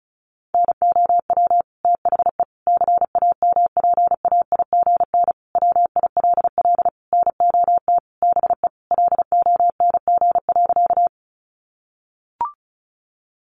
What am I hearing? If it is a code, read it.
Morse code, heard as a keyed tone, about 34 wpm, 700 Hz. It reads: NOW THE CAMPAIGN WILL NOT BE LONG.
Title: Sentences from Top 1200 Words 35wpm